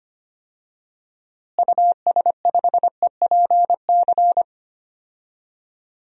Morse code, heard as a keyed tone, about 25 wpm, 700 Hz. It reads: US5EPC
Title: DXCC-100 Call Signs - 6 Characters - Repeated 3x - 25wpm